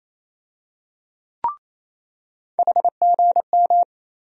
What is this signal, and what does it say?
Morse code, heard as a keyed tone, about 28 wpm, 700 Hz. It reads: HGM
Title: USA Call Sign Postfixes - 3 Letters 28wpm